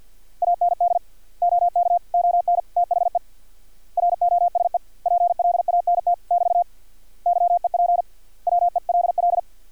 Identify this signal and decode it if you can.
Morse code, heard as a keyed tone, about 50 wpm, 700 Hz. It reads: AND OK ON THE ROSE PLANT <BT> YEP WELL
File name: archived / CRSnet